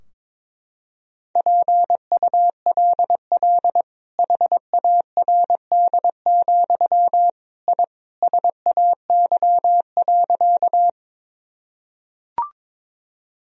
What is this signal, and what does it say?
Morse code, heard as a keyed tone, about 22 wpm, 700 Hz. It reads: PULL HARD, I SAY.
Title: Sentences from Top 1000 Words - Review Part 2 - 22wpm